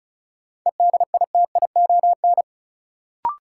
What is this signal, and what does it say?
Morse code, heard as a keyed tone, about 36 wpm, 700 Hz. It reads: EDITION
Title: Sets of 5 Words - New Word Drill - 35wpm